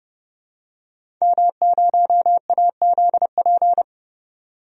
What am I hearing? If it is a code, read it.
Morse code, heard as a keyed tone, about 30 wpm, 700 Hz. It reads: M0AZP
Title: DXCC-100 Call Signs - 5 Characters - 30wpm